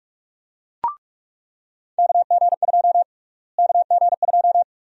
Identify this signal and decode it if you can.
Morse code, heard as a keyed tone, about 46 wpm, 700 Hz. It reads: KG2 KG2
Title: USA Call Sign Prefixes - Repeated 2x - 45wpm